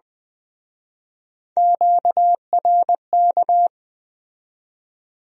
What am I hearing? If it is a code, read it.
Morse code, heard as a keyed tone, about 21 wpm, 700 Hz. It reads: QRK